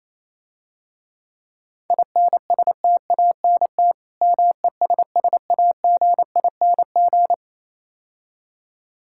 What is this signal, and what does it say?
Morse code, heard as a keyed tone, about 28 wpm, 690 Hz. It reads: INSTANT MESSAGING